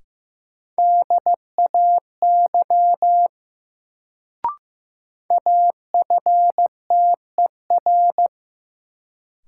Morse code, heard as a keyed tone, about 15 wpm, 700 Hz. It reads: DAY AFTER
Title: Sentences from Top 200 Words - New Word Drill - 15wpm